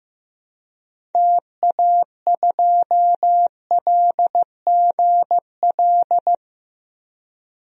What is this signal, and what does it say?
Morse code, heard as a keyed tone, about 15 wpm, 700 Hz. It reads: TA2LGL
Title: DXCC-100 Call Signs - 6 Characters - 15wpm